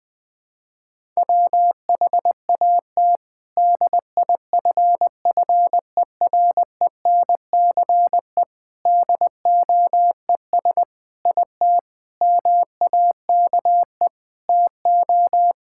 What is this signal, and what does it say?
Morse code, heard as a keyed tone, about 20 wpm, 700 Hz. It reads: WHAT DIFFERENCE DOES IT MAKE TO
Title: Sentences from Top 1000 Words - Review Part 5 - 20wpm